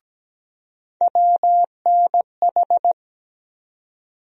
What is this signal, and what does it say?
Morse code, heard as a keyed tone, about 17 wpm, 700 Hz. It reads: WNH
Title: USA Call Sign Postfixes - 3 Letters 17wpm